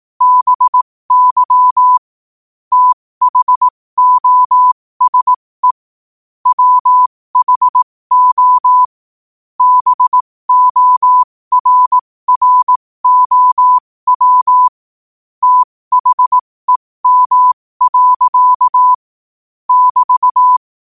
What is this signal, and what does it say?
Morse code, heard as a keyed tone, about 18 wpm, spaced as for 15 wpm, 1000 Hz. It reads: BY THOSE WHO BORROW THEM. <BT>
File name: qotdfiles / 15wpm